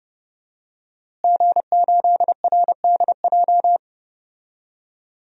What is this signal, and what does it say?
Morse code, heard as a keyed tone, about 30 wpm, 700 Hz. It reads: G8RDJ
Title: DXCC-100 Call Signs - 5 Characters - Repeated 2x - 30wpm